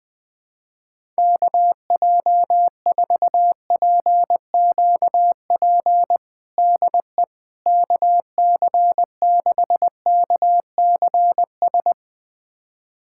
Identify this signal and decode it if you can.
Morse code, heard as a keyed tone, about 20 wpm, 700 Hz. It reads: KJ4PQP DE KC6KCS
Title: USA Callsign DE Callsign - 6 Characters - 20wpm